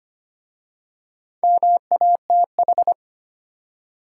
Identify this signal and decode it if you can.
Morse code, heard as a keyed tone, about 25 wpm, 700 Hz. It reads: MATH
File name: practice / 25wpm / Easy Words with Hints - 25wpm